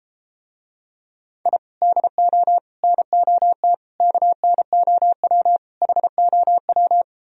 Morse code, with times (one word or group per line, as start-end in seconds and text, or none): 1.5-1.6 I
1.8-2.6 DO
2.8-3.7 NOT
4.0-5.6 KNOW
5.8-7.0 HOW